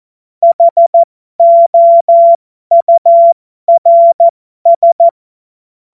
Morse code, heard as a keyed tone, about 14 wpm, spaced as for 12 wpm, 680 Hz. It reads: HOURS
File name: Morsekurs / Witze